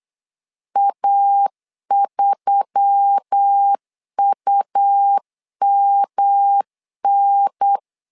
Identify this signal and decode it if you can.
Morse code, heard as a keyed tone, about 8 wpm, 800 Hz. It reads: A3UMN
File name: LEZIONI / 4 Q1S3 8wpm 3037